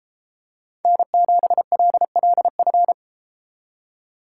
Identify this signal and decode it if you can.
Morse code, heard as a keyed tone, about 32 wpm, 700 Hz. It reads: N7LLF